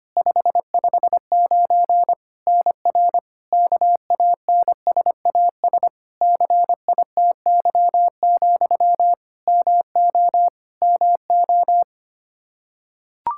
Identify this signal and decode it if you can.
Morse code, heard as a keyed tone, about 25 wpm, 700 Hz. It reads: 559 NR KANSAS CITY, MO MO
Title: K1USN SST QSO Practice 25wpm